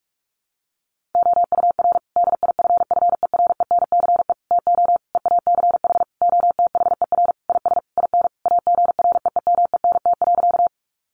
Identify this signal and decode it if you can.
Morse code, heard as a keyed tone, about 46 wpm, 700 Hz. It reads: OUR DIFFERENCE TO EACH OTHER IS IN AGREEMENT.
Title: Sentences from Top 1500 Words 45wpm